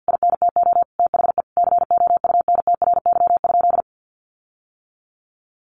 Morse code, heard as a keyed tone, about 50 wpm, 700 Hz. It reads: INTO THE COUNTRY?
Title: Sentences from Top 1400 Words 50wpm